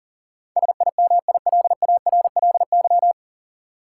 Morse code, heard as a keyed tone, about 40 wpm, 700 Hz. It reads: SIMILARLY
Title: Sentences from Top 1400 Words - New Word Drill Alt - 40wpm